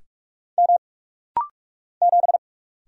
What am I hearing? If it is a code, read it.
Morse code, heard as a keyed tone, about 44 wpm, 700 Hz. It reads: M 7
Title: Single Letter-Number Alt - Rapid-Fire 45wpm